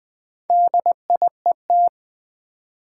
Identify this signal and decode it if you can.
Morse code, heard as a keyed tone, about 20 wpm, 700 Hz. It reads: DIET
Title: Build A Word - 2 Letters At A Time - 4-Letter Words - 20wpm